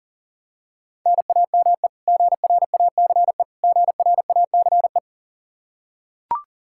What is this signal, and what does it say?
Morse code, heard as a keyed tone, about 40 wpm, 700 Hz. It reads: NAME GRACE GRACE
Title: Practice QSO - Exchange 1 - Beginning - 40wpm